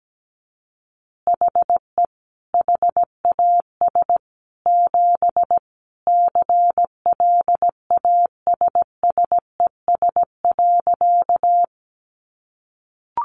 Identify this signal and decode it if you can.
Morse code, heard as a keyed tone, about 17 wpm, 700 Hz. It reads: HE HAS 7 CLASSES.
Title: Additional Easy Phrases 17wpm